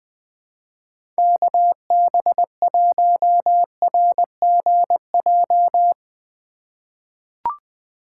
Morse code, heard as a keyed tone, about 20 wpm, 700 Hz. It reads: KB1RGJ
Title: USA Call Signs - 3x Character Spacing - 20wpm